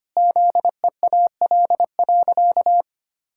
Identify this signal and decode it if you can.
Morse code, heard as a keyed tone, about 25 wpm, 700 Hz. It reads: ZEAL.
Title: Pangram Challenge v2 - 25wpm